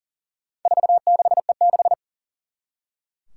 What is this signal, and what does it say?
Morse code, heard as a keyed tone, about 40 wpm, 700 Hz. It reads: 46E6
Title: Four Challenging Letter-Number Combinations 40wpm